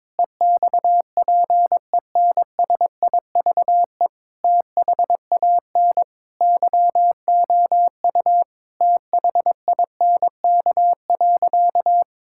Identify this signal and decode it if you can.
Morse code, heard as a keyed tone, about 22 wpm, 700 Hz. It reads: EXPENSIVE THAN YOU THINK.
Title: Sentences from Top 1000 Words - Review Part 6 - 22wpm